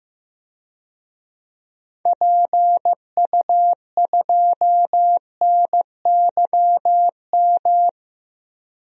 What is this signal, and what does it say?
Morse code, heard as a keyed tone, about 15 wpm, 700 Hz. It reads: PU2NYM
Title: DXCC-100 Call Signs - 6 Characters - 15wpm